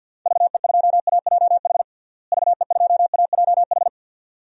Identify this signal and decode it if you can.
Morse code, heard as a keyed tone, about 52 wpm, 700 Hz. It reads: VE2AJH VE2AJH
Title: DXCC-100 Call Signs - 6 Characters - Repeated 2x - 50wpm